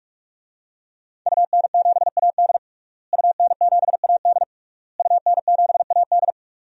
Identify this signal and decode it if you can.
Morse code, heard as a keyed tone, about 46 wpm, 700 Hz. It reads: UN7AD UN7AD UN7AD
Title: DXCC-100 Call Signs - 5 Characters - Repeated 3x - 45wpm